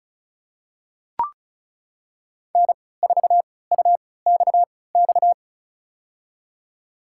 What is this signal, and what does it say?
Morse code, heard as a keyed tone, about 36 wpm, spaced as for 20 wpm, 690 Hz. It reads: N4UXX
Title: USA Call Signs - 3x Character Spacing - 35wpm